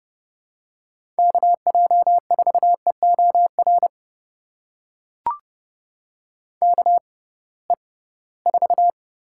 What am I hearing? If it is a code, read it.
Morse code, heard as a keyed tone, about 30 wpm, spaced as for 19 wpm, 700 Hz. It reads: KJ4EOR KE4
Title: USA Call Signs - 6 Characters - 6x Character Spacing - 30wpm